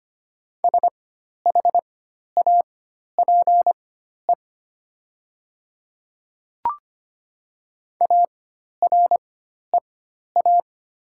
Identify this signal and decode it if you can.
Morse code, heard as a keyed tone, about 25 wpm, spaced as for 12 wpm, 700 Hz. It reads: SHAPE AREA
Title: 4x Character Spacing - Top 500 Words - 25wpm